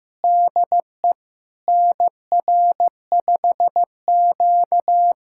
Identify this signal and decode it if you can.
Morse code, heard as a keyed tone, about 16 wpm, 700 Hz. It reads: DE NR5Q
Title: USA Callsign DE Callsign - 4 Characters - 15wpm